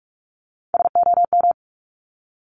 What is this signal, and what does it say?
Morse code, heard as a keyed tone, about 46 wpm, 690 Hz. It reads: SOM
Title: Top Three Letter Combinations in Words 45wpm